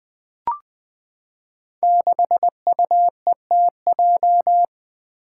Morse code, heard as a keyed tone, about 20 wpm, 700 Hz. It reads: 6UETJ
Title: Five Challenging Letter-Number Combinations 20wpm